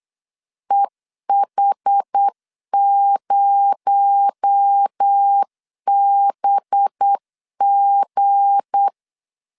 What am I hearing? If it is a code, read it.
Morse code, heard as a keyed tone, about 8 wpm, 800 Hz. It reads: EH0BG
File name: LEZIONI / 7 GF9Z / 7 GF 8wpm 3108